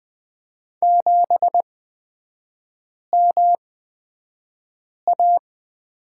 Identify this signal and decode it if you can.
Morse code, heard as a keyed tone, about 20 wpm, spaced as for 5 wpm, 700 Hz. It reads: 7MA
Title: 250 Lesson 40e - BK - Three Characters 20wpm